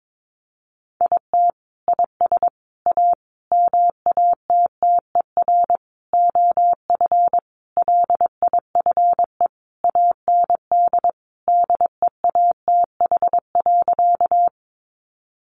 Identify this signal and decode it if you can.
Morse code, heard as a keyed tone, about 22 wpm, 700 Hz. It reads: IT IS A MATTER OF LIFE AND DEATH.
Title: Sentences from Top 1100 Words 22wpm